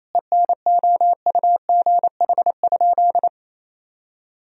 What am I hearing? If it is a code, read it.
Morse code, heard as a keyed tone, about 28 wpm, 700 Hz. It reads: ENOUGH?
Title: Sentences from Top 1000 Words - Review Part 3 - 28wpm